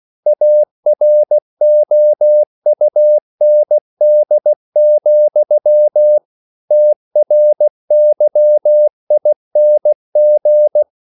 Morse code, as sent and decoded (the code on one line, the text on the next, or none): .- .-. --- ..- -. -.. --..-- / - .-. -.-- .. -. --.
AROUND, TRYING